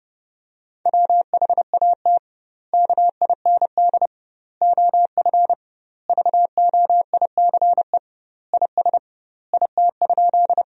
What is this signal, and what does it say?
Morse code, heard as a keyed tone, about 30 wpm, 700 Hz. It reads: WHAT KIND OF VOICE IS IT?
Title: 2x Word-Spacing - Sentences from Top 500 Words - 30wpm